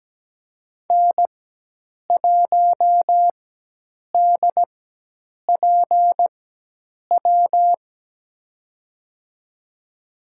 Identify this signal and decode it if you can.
Morse code, heard as a keyed tone, about 17 wpm, spaced as for 8 wpm, 700 Hz. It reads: N1DPW
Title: USA Call Signs - 4x Character Spacing - 17wpm